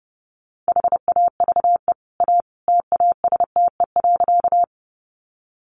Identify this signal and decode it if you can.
Morse code, heard as a keyed tone, about 28 wpm, 700 Hz. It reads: HAVE A TASTE.
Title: Sentences from Top 1100 Words 30wpm